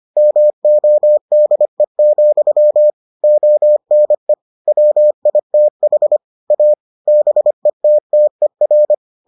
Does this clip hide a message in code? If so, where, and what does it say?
Yes: Morse code from 0.2 to 8.9 s, heard as a keyed tone, about 25 wpm, 600 Hz. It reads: MODE, ONE WITH A BETTER